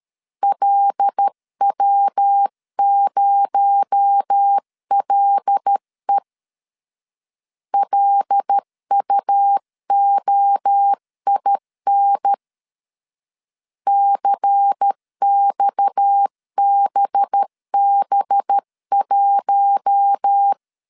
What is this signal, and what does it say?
Morse code, heard as a keyed tone, about 13 wpm, 800 Hz. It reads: LW0LE LUOIN CXBB1
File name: CW / DET / 1 6 1500C S C 12wpm 3533